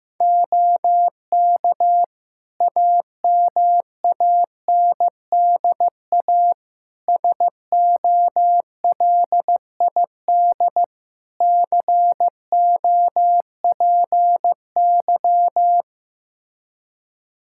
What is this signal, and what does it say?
Morse code, heard as a keyed tone, about 15 wpm, 700 Hz. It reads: OK AMANDA SOLID COPY
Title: K1USN SST QSO Practice 15wpm